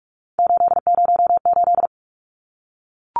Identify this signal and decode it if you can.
Morse code, heard as a keyed tone, about 44 wpm, 700 Hz. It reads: 808
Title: Numbers - 3-Digit 45wpm